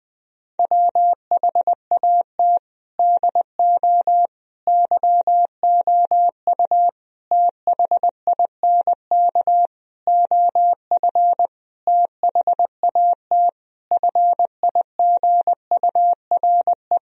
Morse code, heard as a keyed tone, about 19 wpm, 700 Hz. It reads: WHAT DO YOU THINK OF THAT FIGURE
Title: Sentences from Top 1000 Words - Review Part 2 - 20wpm